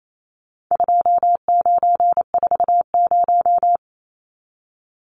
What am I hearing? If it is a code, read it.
Morse code, heard as a keyed tone, about 28 wpm, 700 Hz. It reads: 2940
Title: Numbers - 4-Digit 28wpm